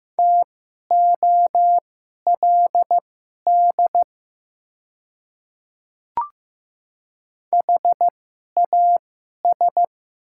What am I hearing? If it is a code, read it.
Morse code, heard as a keyed tone, about 15 wpm, spaced as for 11 wpm, 700 Hz. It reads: TOLD HAS
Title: 2x Character Spacing - Top 500 Words - 15wpm